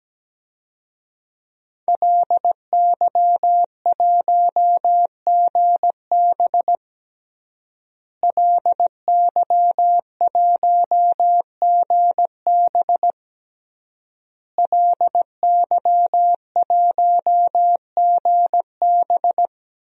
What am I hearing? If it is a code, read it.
Morse code, heard as a keyed tone, about 17 wpm, 700 Hz. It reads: LY1GB LY1GB LY1GB
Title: DXCC-100 Call Signs - 5 Characters - Repeated 3x - 17wpm